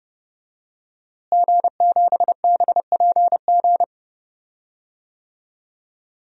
Morse code, heard as a keyed tone, about 30 wpm, 700 Hz. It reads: G7BPG
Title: DXCC-100 Call Signs - 5 Characters - Repeated 2x - 30wpm